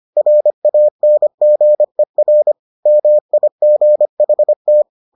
Morse code, heard as a keyed tone, about 25 wpm, 600 Hz. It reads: RANGER MIGHT